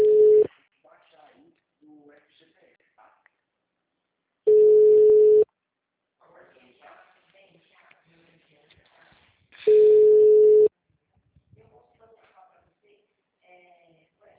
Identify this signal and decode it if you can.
Ringback tone (Europe, 425 Hz), with speech.
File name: out- -2009- - - .20582